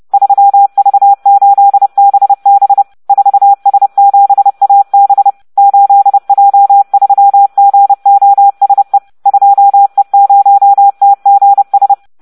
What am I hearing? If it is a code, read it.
Morse code, heard as a keyed tone, about 30 wpm, 800 Hz. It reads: 3V8BB 4S7AB 8J3GOSE 2E0TGS